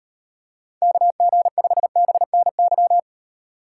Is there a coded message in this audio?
Yes: Morse code from 0.8 to 3.0 s, heard as a keyed tone, about 38 wpm, 700 Hz. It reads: KG5BNY